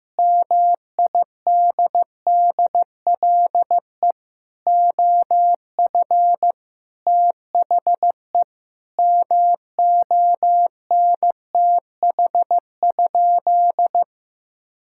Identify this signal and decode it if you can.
Morse code, heard as a keyed tone, about 16 wpm, 700 Hz. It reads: MIDDLE OF THE MONTH?
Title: Sentences from Top 1000 Words - Review Part 6 - 15wpm